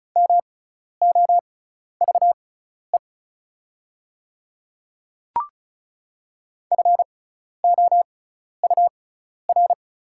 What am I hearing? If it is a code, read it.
Morse code, heard as a keyed tone, about 34 wpm, spaced as for 12 wpm, 690 Hz. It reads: MOVE FOUR
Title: 6x Character Spacing - Top 300 Words - 35wpm